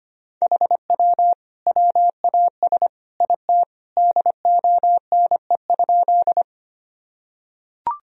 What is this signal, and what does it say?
Morse code, heard as a keyed tone, about 25 wpm, 700 Hz. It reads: HW WAS IT DONE?
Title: Sentences from Top 500 Words and Abbreviations 25wpm